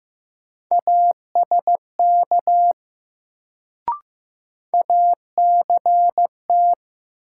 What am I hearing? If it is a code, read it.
Morse code, heard as a keyed tone, about 15 wpm, 700 Hz. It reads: ASK ACT
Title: Sentences from Top 200 Words - New Word Drill - 15wpm